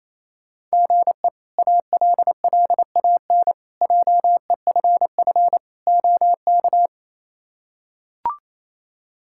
Morse code, heard as a keyed tone, about 27 wpm, 700 Hz. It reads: GE ALLAN JEFF OK
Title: K1USN SST QSO Practice - 28wpm